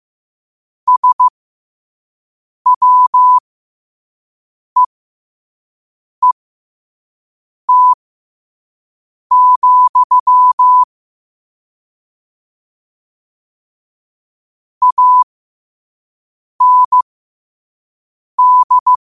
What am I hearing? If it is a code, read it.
Morse code, heard as a keyed tone, about 15 wpm, spaced as for 5 wpm, 1000 Hz. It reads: SWEET, AND
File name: qotdfiles / 5wpm